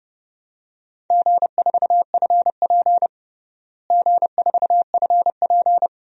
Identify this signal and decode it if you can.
Morse code, heard as a keyed tone, about 32 wpm, 700 Hz. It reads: G4FP G4FP
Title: DXCC-100 Call Signs - 4 Characters - Repeated 3x - 30wpm